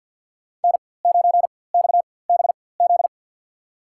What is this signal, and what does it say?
Morse code, heard as a keyed tone, about 52 wpm, spaced as for 24 wpm, 700 Hz. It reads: N9XBZ
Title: USA Call Signs - 5 Characters - 4x Character Spacing - 50wpm